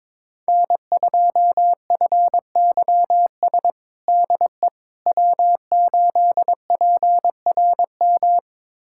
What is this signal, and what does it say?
Morse code, heard as a keyed tone, about 22 wpm, 700 Hz. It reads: N2FYS DE W8PRM
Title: USA Callsign DE Callsign - 5 Characters - 22wpm